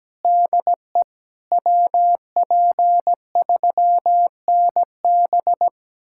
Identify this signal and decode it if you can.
Morse code, heard as a keyed tone, about 17 wpm, 700 Hz. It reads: DE WP3NB
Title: USA Callsign DE Callsign - 5 Characters - 17wpm